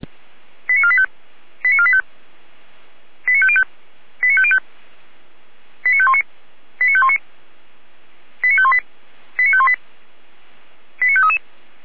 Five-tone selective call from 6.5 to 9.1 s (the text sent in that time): 89519 89518